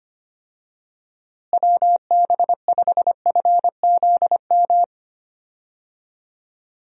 Morse code, heard as a keyed tone, about 25 wpm, 700 Hz. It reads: WB5FZM
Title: USA Call Signs - 6 Characters - Repeated 3x - 25wpm